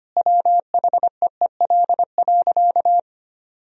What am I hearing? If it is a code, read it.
Morse code, heard as a keyed tone, about 25 wpm, 700 Hz. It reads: WHEEL.
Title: Sentences from Top 1000 Words - Review Part 3 - 25wpm